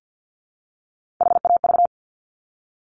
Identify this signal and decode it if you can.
Morse code, heard as a keyed tone, about 50 wpm, 700 Hz. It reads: HAV
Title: Top Three Letter Combinations in Words 50wpm